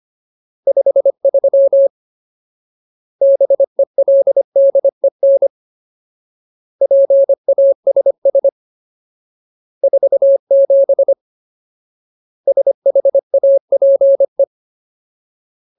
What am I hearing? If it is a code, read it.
Morse code, heard as a keyed tone, about 25 wpm, 550 Hz. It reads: 53 BELDEN PASS 47 SHAPE
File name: Ultimate Word List / 25WPM Ultimate